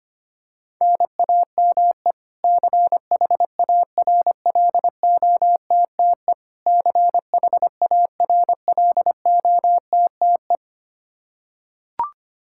Morse code, heard as a keyed tone, about 25 wpm, 700 Hz. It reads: NAME CHARLOTTE CHARLOTTE
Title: Practice QSO - Exchange 1 - Beginning - 25wpm